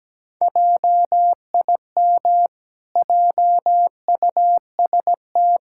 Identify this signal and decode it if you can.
Morse code, heard as a keyed tone, about 17 wpm, 700 Hz. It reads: JIM JUST
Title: Pangram Challenge v2 - 17wpm